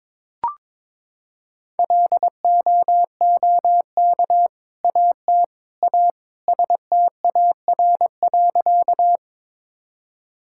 Morse code, heard as a keyed tone, about 22 wpm, 700 Hz. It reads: LOOK AT A STAR.
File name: practice / 22wpm / Sentences with 2-Word Phrases - 22wpm